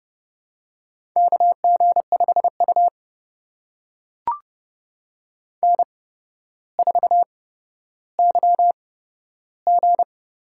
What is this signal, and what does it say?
Morse code, heard as a keyed tone, about 31 wpm, spaced as for 13 wpm, 700 Hz. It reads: KG5U N4YG
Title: USA Call Signs - 4 Characters - 8x Character Spacing - 30wpm